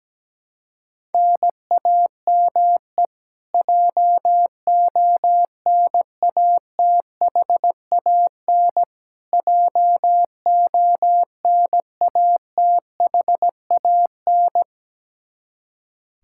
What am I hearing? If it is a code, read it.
Morse code, heard as a keyed tone, about 17 wpm, 700 Hz. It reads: NAME JONATHAN JONATHAN
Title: Practice QSO - Exchange 1 - Beginning - 17wpm